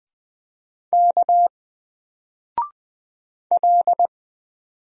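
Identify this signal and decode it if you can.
Morse code, heard as a keyed tone, about 20 wpm, 700 Hz. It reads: K L
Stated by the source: Single letter and number Morse drill at 20 wpm, from N4LCD.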